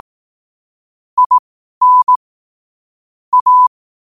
Morse code, heard as a keyed tone, about 18 wpm, spaced as for 12 wpm, 1000 Hz. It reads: IN A
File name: qotdfiles / 12wpm